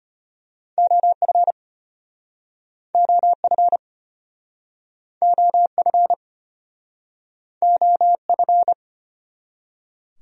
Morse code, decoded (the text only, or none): OF OF OF OF